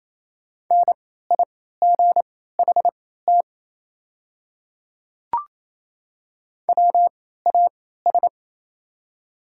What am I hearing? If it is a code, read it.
Morse code, heard as a keyed tone, about 28 wpm, spaced as for 16 wpm, 700 Hz. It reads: NIGHT WAS
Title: 3x Character Spacing - Top 300 Words - 28wpm